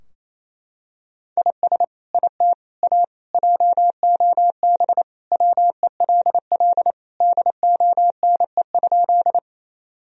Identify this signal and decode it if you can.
Morse code, heard as a keyed tone, about 27 wpm, 700 Hz. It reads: IS IT A JOB WELL DONE?
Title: Sentences from Top 1000 Words - Review Part 3 - 28wpm